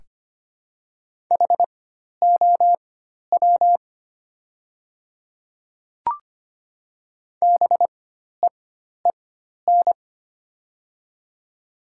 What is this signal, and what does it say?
Morse code, heard as a keyed tone, about 25 wpm, spaced as for 12 wpm, 700 Hz. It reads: HOW BEEN
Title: 4x Character Spacing - Top 100 Words - 25wpm